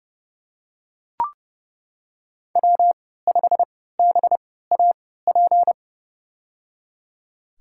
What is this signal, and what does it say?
Morse code, heard as a keyed tone, about 30 wpm, spaced as for 17 wpm, 700 Hz. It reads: W5BAP
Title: USA Call Signs - 5 Characters - 3x Character Spacing - 30wpm